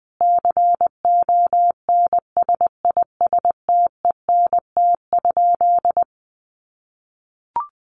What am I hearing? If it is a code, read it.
Morse code, heard as a keyed tone, about 20 wpm, 700 Hz. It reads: CONSISTENT?
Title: Sentences from Top 1500 Words 20wpm